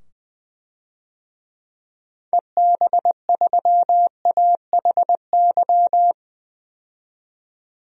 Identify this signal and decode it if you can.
Morse code, heard as a keyed tone, about 20 wpm, 700 Hz. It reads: EB3AHY
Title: DXCC-100 Call Signs - 6 Characters - 20wpm